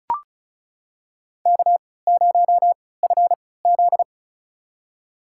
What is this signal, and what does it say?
Morse code, heard as a keyed tone, about 34 wpm, spaced as for 20 wpm, 700 Hz. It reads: K0FZ
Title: USA Call Signs - 4 Characters - 3x Character Spacing - 35wpm